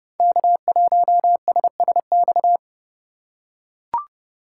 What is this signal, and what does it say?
Morse code, heard as a keyed tone, about 28 wpm, 700 Hz. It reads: K1SSX
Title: Code Groups - 5 Letter:Numbers - 30wpm